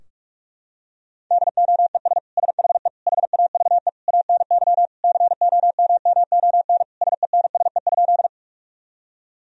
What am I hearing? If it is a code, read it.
Morse code, heard as a keyed tone, about 41 wpm, 700 Hz. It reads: DOES SHE HAVE ANY COMMON SENSE?
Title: Sentences from Top 1000 Words - Review Part 4 - 45wpm